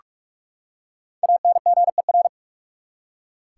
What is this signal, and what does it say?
Morse code, heard as a keyed tone, about 46 wpm, 700 Hz. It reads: ANGER
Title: Five Letter Words - 45wpm